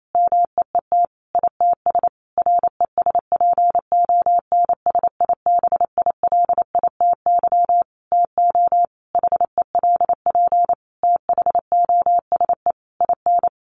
Morse code, heard as a keyed tone, about 28 wpm, 700 Hz. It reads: MEET ITS RESPONSIBILITY TO HELP THOSE IN
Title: Sentences from Top 1200 Words 28wpm